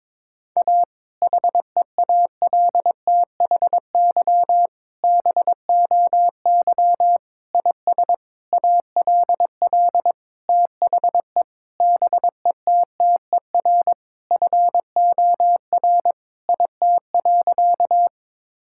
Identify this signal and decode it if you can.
Morse code, heard as a keyed tone, about 22 wpm, 700 Hz. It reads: A HEALTHY BOY IS ALL THE BETTER FOR IT.
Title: Sentences from Top 1000 Words - Review Part 5 - 22wpm